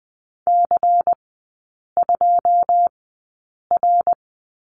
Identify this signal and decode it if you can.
Morse code, heard as a keyed tone, about 20 wpm, spaced as for 8 wpm, 700 Hz. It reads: C2R
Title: 093 Lesson 16e - Letter c - Three Characters 20wpm